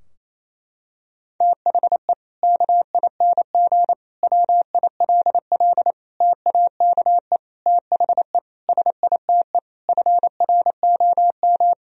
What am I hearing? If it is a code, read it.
Morse code, heard as a keyed tone, about 28 wpm, 700 Hz. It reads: THE KING WILL TAKE THE SITE FROM
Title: Sentences from Top 1000 Words - Review Part 4 - 28wpm